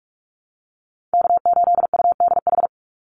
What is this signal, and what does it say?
Morse code, heard as a keyed tone, about 44 wpm, 700 Hz. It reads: K8UDH
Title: Call Signs Repeated 3x - 45wpm